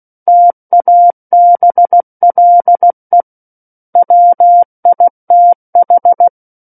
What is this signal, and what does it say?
Morse code, heard as a keyed tone, about 16 wpm, 700 Hz. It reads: TABLE WITH
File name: Book One / 07-Chapter 7 - 16 WPM